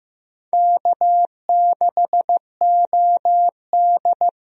Morse code, heard as a keyed tone, about 16 wpm, 700 Hz. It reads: K6OD
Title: USA Call Signs - 4 Characters - 15wpm